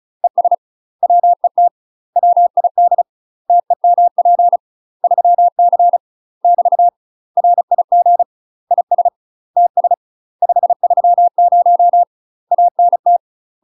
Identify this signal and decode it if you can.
Morse code, heard as a keyed tone, about 34 wpm, 700 Hz. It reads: ES WET WID TEMP 3C <BT> RIG IS TS 530 ANT
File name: UK MASTER QSOs TRAINING / 35WPM / UK MASTER QSOs 0001